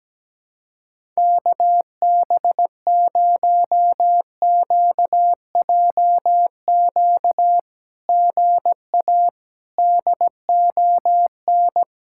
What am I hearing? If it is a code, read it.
Morse code, heard as a keyed tone, about 17 wpm, 700 Hz. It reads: KB0QJQ GA DON
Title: SOTA QSO Practice 17wpm